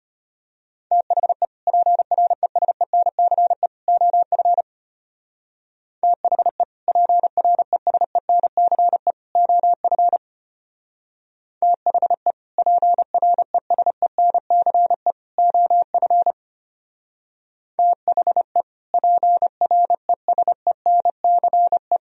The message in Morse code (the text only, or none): THE PRESENCE OF THE PRESENCE OF THE PRESENCE OF THE PRESENCE